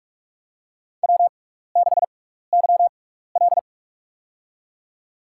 Morse code, heard as a keyed tone, about 46 wpm, spaced as for 15 wpm, 700 Hz. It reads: W6YL
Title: USA Call Signs - 4 Characters - 6x Character Spacing - 45wpm